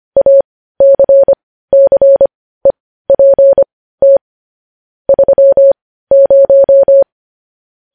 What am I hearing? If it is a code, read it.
Morse code, heard as a keyed tone, about 25 wpm, spaced as for 15 wpm, 550 Hz. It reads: ACCEPT 30
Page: News Headlines in Morse Code at 15 WPM